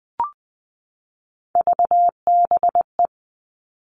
Morse code, heard as a keyed tone, about 20 wpm, 700 Hz. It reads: VBE
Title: All Cut Numbers - 3-Digit 20wpm